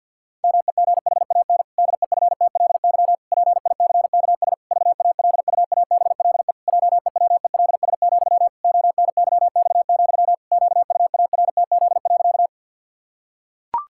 Morse code, heard as a keyed tone, about 50 wpm, 700 Hz. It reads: MEGHAN DEFTLY PICKS VALUABLE JEWELS, ONYX, QUARTZ.
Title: Pangram Challenge v2 - 50wpm